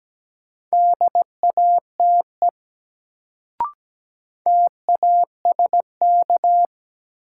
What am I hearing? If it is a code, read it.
Morse code, heard as a keyed tone, about 17 wpm, 700 Hz. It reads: DATE TASK
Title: Sentences from Top 1000 Words - Review Part 7 - New Word Drill - 17wpm